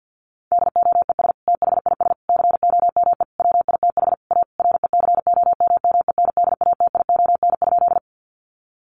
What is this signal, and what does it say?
Morse code, heard as a keyed tone, about 48 wpm, 700 Hz. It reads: DOES THIS COME WITH A RECOMMENDATION?
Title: Sentences from Top 1400 Words 50wpm